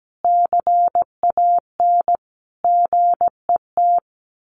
Morse code, heard as a keyed tone, about 17 wpm, 700 Hz. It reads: CAN GET
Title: Sets of 5 Words Encore2 17wpm